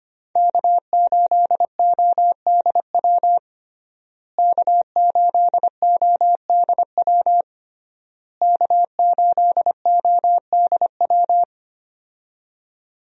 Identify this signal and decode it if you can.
Morse code, heard as a keyed tone, about 25 wpm, 700 Hz. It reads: K8ODW K8ODW K8ODW
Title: USA Call Signs - 5 Characters - Repeated 3x - 25wpm